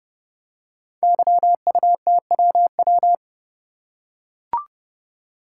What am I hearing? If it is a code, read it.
Morse code, heard as a keyed tone, about 30 wpm, 700 Hz. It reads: YUTWW
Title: Code Groups - 5 Letters - 30wpm